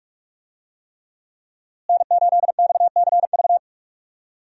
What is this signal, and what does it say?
Morse code, heard as a keyed tone, about 46 wpm, 700 Hz. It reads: N8XCV
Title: USA Call Signs - 5 Characters - Repeated 3x - 45wpm